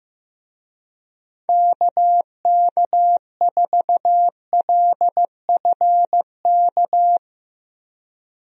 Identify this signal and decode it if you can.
Morse code, heard as a keyed tone, about 15 wpm, 700 Hz. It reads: KK4LFK
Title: USA Call Signs - 6 Characters - Repeated 3x - 15wpm